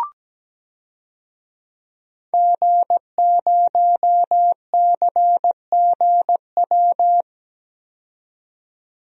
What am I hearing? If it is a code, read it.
Morse code, heard as a keyed tone, about 17 wpm, 700 Hz. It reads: G0CGW
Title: DXCC-100 Call Signs - 5 Characters - 17wpm